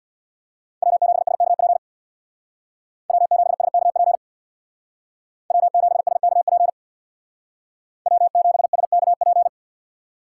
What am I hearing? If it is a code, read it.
Morse code, heard as a keyed tone, about 66 wpm, 700 Hz. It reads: W7SKP W7SKP W7SKP W7SKP